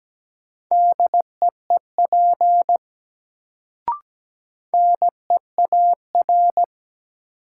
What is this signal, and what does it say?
Morse code, heard as a keyed tone, about 17 wpm, 700 Hz. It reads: DEEP NEAR
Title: Sets of 5 Words Encore - New Word Drill - 17wpm